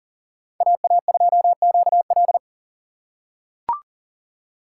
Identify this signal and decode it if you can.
Morse code, heard as a keyed tone, about 40 wpm, 700 Hz. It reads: AA2QL
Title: USA Call Signs - 5 Characters - 4x Character Spacing - 40wpm